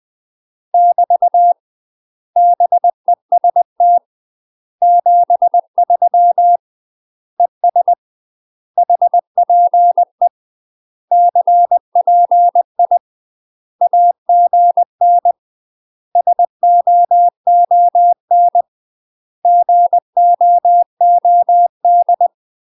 Morse code, heard as a keyed tone, about 20 wpm, 700 Hz. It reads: <BT> BEST 73 ES HPE CPI AGN SOON GOOD
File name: UK MASTER QSOs TRAINING / 20WPM / UK MASTER QSOs 0064